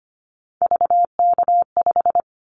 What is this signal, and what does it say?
Morse code, heard as a keyed tone, about 25 wpm, 700 Hz. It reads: VK5
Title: Callsign Prefixes for Top 50 DXCC - 25wpm